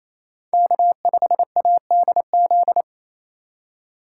Morse code, heard as a keyed tone, about 28 wpm, 700 Hz. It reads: K5ADZ
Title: USA Call Signs - 5 Characters - 2x Character Spacing - 28wpm